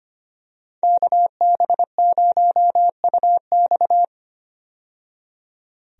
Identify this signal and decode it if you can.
Morse code, heard as a keyed tone, about 25 wpm, 690 Hz. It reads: KB0UX
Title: USA Call Signs - 5 Characters - 25wpm